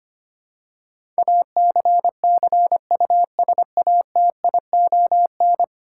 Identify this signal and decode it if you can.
Morse code, heard as a keyed tone, about 25 wpm, 700 Hz. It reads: ACCUSATION
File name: Ten Letter Words - 25wpm